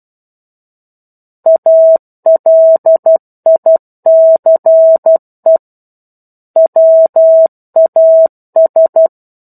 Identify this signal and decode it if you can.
Morse code, heard as a keyed tone, about 12 wpm, 650 Hz. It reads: ALICE WAS
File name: 01-Chapter-01 12 WPM